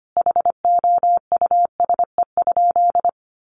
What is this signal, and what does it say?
Morse code, heard as a keyed tone, about 25 wpm, 700 Hz. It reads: HOUSE?
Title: Sentences from Top 700 Words 25wpm